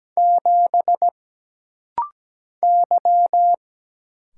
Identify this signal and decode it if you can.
Morse code, heard as a keyed tone, about 17 wpm, 700 Hz. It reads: 7 Y
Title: Single Letter-Number Alt - Rapid-Fire 17wpm